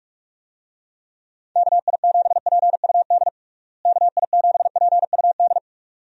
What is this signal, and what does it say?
Morse code, heard as a keyed tone, about 46 wpm, 700 Hz. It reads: KI7PUD KI7PUD
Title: USA Call Signs - 6 Characters - Repeated 2x - 45wpm